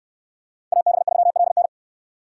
Morse code, heard as a keyed tone, about 69 wpm, 690 Hz. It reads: AB3LN